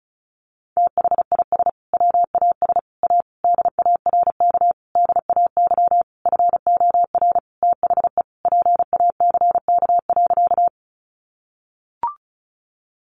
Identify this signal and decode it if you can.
Morse code, heard as a keyed tone, about 34 wpm, 700 Hz. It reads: THIS WAS A DARK DAY FOR THE PACK.
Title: Sentences from Top 1300 Words 35wpm